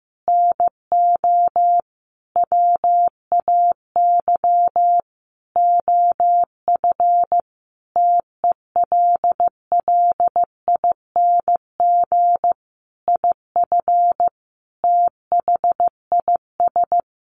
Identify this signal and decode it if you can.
Morse code, heard as a keyed tone, about 15 wpm, 700 Hz. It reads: NO WAY OF TELLING IF THIS
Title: A Sound Of Thunder 15wpm